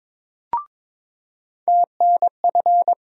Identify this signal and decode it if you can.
Morse code, heard as a keyed tone, about 22 wpm, 700 Hz. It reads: TNF
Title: USA Call Sign Postfixes - 3 Letters 22wpm